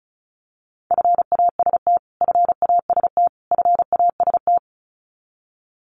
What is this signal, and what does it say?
Morse code, heard as a keyed tone, about 34 wpm, 700 Hz. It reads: FAST FAST FAST
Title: Top 500 Words Repeated 35wpm